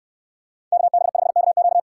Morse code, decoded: KB5KQ

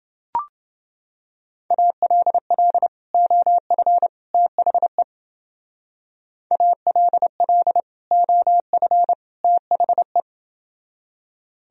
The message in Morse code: ALL OF THE ALL OF THE

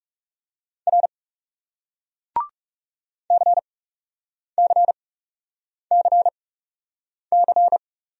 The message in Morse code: R CCCC

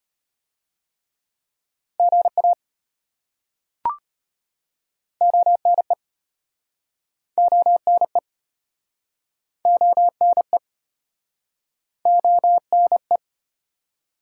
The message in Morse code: GA ONE ONE ONE ONE

